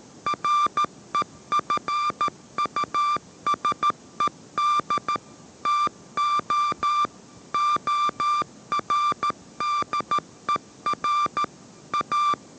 Morse code, heard as a keyed tone, about 13 wpm, 1180 Hz. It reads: REFUSED TO ORDER A